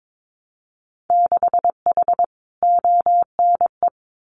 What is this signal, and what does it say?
Morse code, heard as a keyed tone, about 22 wpm, 700 Hz. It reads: 6H ONE
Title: Winter Field Day Exchanges Encore 22wpm